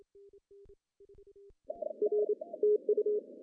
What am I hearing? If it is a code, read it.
Morse code, heard as a keyed tone, about 29 wpm, 420 Hz. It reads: R TU
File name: CW / cqww-cw-k5tr